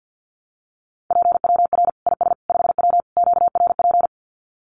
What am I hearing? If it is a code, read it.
Morse code, heard as a keyed tone, about 50 wpm, 700 Hz. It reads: PWR IS 5W QRP
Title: Common QSO Phrases 50wpm